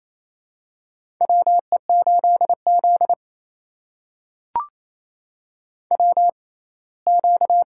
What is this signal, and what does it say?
Morse code, heard as a keyed tone, about 28 wpm, spaced as for 19 wpm, 700 Hz. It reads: WE8Z WQ
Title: USA Call Signs - 4 Characters - 6x Character Spacing - 28wpm